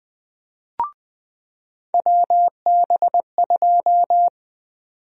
Morse code, heard as a keyed tone, about 20 wpm, 700 Hz. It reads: WB2
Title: USA Call Sign Prefixes - 20wpm